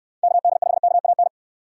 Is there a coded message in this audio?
Yes: Morse code from 0.2 to 1.3 s, heard as a keyed tone, about 68 wpm, 700 Hz. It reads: KD5KAN